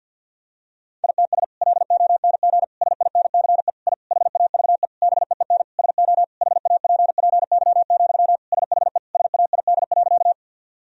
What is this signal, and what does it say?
Morse code, heard as a keyed tone, about 48 wpm, 700 Hz. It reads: ITS LONG SINCE I HAVE BEEN SO HAPPY, SHE SAID.
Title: Sentences from Top 1000 Words - Review Part 1 - 50wpm